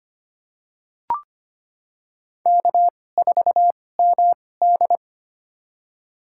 Morse code, heard as a keyed tone, about 25 wpm, spaced as for 18 wpm, 700 Hz. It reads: K4MD
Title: USA Call Signs - 4 Characters - 2x Character Spacing - 25wpm